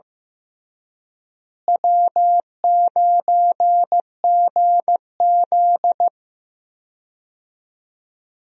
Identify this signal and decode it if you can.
Morse code, heard as a keyed tone, about 15 wpm, 700 Hz. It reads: W9GZ
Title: USA Call Signs - 4 Characters - Repeated 3x - 15wpm